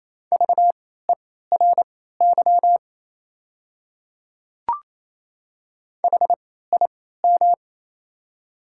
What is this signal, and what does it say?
Morse code, heard as a keyed tone, about 28 wpm, spaced as for 16 wpm, 700 Hz. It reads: VERY HIM
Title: 3x Character Spacing - Top 200 Words - 28wpm